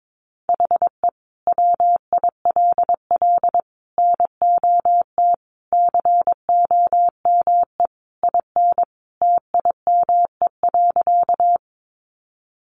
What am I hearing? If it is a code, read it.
Morse code, heard as a keyed tone, about 22 wpm, 700 Hz. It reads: HE WILL NOT COME IN TIME.
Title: Sentences from Top 600 Words 22wpm